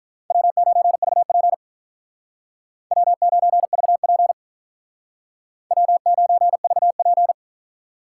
Morse code, heard as a keyed tone, about 45 wpm, 700 Hz. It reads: W9VP W9VP W9VP